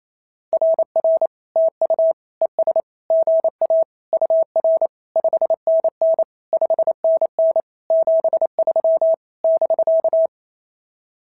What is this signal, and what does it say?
Morse code, heard as a keyed tone, about 25 wpm, 650 Hz. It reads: RR TU ES GA UR 5NN 5NN 73 <BK>
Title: SOTA QSO Practice 28wpm